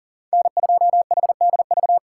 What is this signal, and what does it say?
Morse code, heard as a keyed tone, about 40 wpm, 700 Hz. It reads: N2HDV
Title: USA Call Signs - 5 Characters - 2x Character Spacing - 40wpm